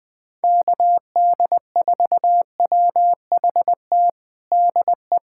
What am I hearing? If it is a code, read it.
Morse code, heard as a keyed tone, about 21 wpm, 700 Hz. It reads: KD4WHT DE
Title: USA Callsign DE Callsign - 6 Characters - 20wpm